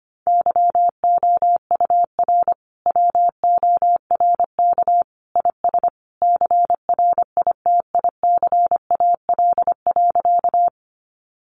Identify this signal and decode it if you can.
Morse code, heard as a keyed tone, about 25 wpm, 700 Hz. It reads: YOUR WORK IS CRITICAL.
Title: Sentences from Top 1100 Words 25wpm